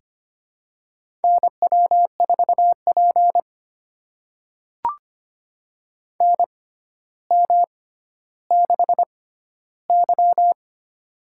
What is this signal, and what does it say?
Morse code, heard as a keyed tone, about 25 wpm, spaced as for 13 wpm, 700 Hz. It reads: NW4P NM6Y
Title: USA Call Signs - 4 Characters - 6x Character Spacing - 25wpm